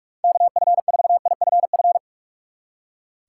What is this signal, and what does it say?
Morse code, heard as a keyed tone, about 57 wpm, 700 Hz. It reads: KF4IFF